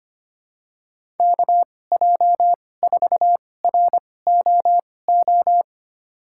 Morse code, heard as a keyed tone, about 25 wpm, spaced as for 18 wpm, 700 Hz. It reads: KJ4ROO
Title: USA Call Signs - 6 Characters - 2x Character Spacing - 25wpm